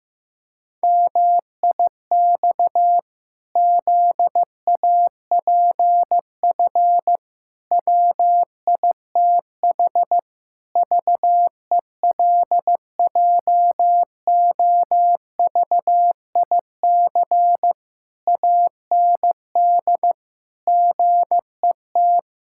Morse code, heard as a keyed tone, about 15 wpm, 700 Hz. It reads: MIX ZAPF WITH VELJOVIC AND GET
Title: Pangram Challenge v2 - 15wpm